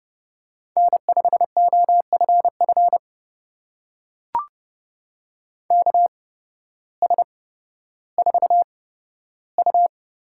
Morse code, heard as a keyed tone, about 30 wpm, spaced as for 14 wpm, 700 Hz. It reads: N5OFF KS4U
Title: USA Call Signs - 5 Characters - 8x Character Spacing - 30wpm